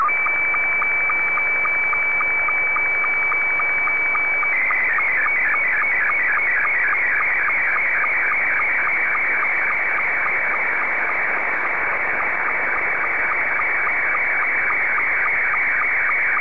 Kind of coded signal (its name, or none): SSTV (slow-scan television)